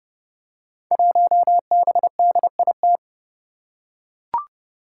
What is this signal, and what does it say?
Morse code, heard as a keyed tone, about 30 wpm, 700 Hz. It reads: 1BDIT
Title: Five Challenging Letter-Number Combinations 30wpm